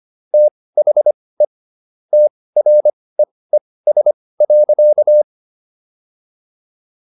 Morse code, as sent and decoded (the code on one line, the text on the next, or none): - .... / - .-. . ... .-.-.-
THE TREES.